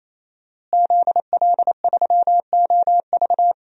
Morse code, heard as a keyed tone, about 28 wpm, 700 Hz. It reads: ZL3OV
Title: DXCC-100 Call Signs - 5 Characters - Repeated 3x - 28wpm